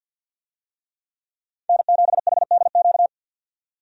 Morse code, heard as a keyed tone, about 50 wpm, 700 Hz. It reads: N7HDQ
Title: USA Call Signs - 5 Characters - 50wpm